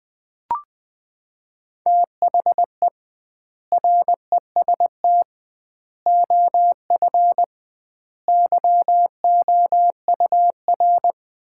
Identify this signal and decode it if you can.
Morse code, heard as a keyed tone, about 20 wpm, 700 Hz. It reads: THE REST OF YOUR
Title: 2x Word-Spacing - Sets of 5 Words - 20wpm